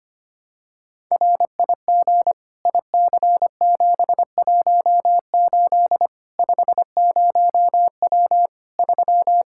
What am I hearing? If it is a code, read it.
Morse code, heard as a keyed tone, about 25 wpm, 700 Hz. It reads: RIG IC718 50W 3
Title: Practice QSO - Exchange 2 - Info - 25wpm